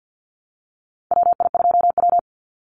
Morse code, heard as a keyed tone, about 50 wpm, 700 Hz. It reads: WI2W
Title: NA SOTA Chaser Callsigns 50wpm